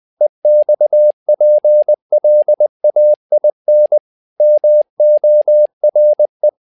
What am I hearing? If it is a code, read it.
Morse code, heard as a keyed tone, about 20 wpm, 600 Hz. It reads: EXPLAIN MORE
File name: learning center / tower trouble / tower trouble 20wpm